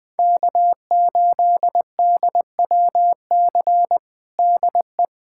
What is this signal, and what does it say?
Morse code, heard as a keyed tone, about 20 wpm, 700 Hz. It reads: K8DWC DE
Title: USA Callsign DE Callsign - 5 Characters - 20wpm